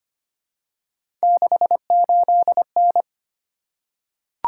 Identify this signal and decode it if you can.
Morse code, heard as a keyed tone, about 25 wpm, 700 Hz. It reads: 68N
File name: Three Letter-Number Combinations - 25wpm